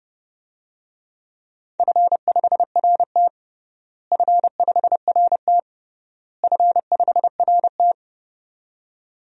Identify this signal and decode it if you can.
Morse code, heard as a keyed tone, about 30 wpm, 700 Hz. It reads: F5RT F5RT F5RT
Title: DXCC-100 Call Signs - 4 Characters - Repeated 3x - 30wpm